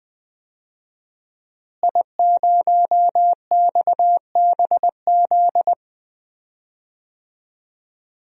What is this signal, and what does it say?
Morse code, heard as a keyed tone, about 20 wpm, 700 Hz. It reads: I0XBZ
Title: DXCC-100 Call Signs - 5 Characters - Repeated 2x - 20wpm